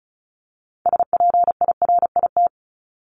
Morse code, heard as a keyed tone, about 34 wpm, 700 Hz. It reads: SPIRIT
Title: Top 1251-1500 Words 35wpm